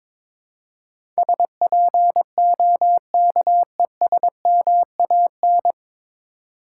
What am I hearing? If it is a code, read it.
Morse code, heard as a keyed tone, about 22 wpm, 700 Hz. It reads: SPOKESMAN
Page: Morse Code Ninja practice 22wpm